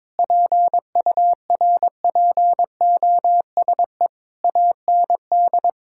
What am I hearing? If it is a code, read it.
Morse code, heard as a keyed tone, about 22 wpm, 700 Hz. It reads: PURPOSE AND
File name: Sentences with 9-Word Phrases - 22wpm